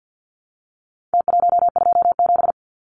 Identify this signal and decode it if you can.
Morse code, heard as a keyed tone, about 49 wpm, 700 Hz. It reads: T127
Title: Numbers - Contest Style - 4-Digit 50wpm